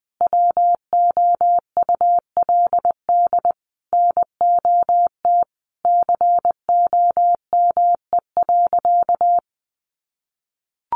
Morse code, heard as a keyed tone, about 20 wpm, 700 Hz. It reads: WOULD NOT COME.
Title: Sentences from Top 1500 Words 20wpm